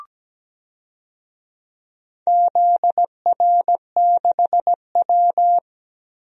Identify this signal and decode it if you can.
Morse code, heard as a keyed tone, about 17 wpm, 700 Hz. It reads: ZR6W
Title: DXCC-100 Call Signs - 4 Characters - 17wpm